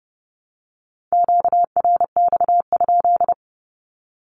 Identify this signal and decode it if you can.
Morse code, heard as a keyed tone, about 28 wpm, 700 Hz. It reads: QRX?
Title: Q-Codes Repeated 3x 30wpm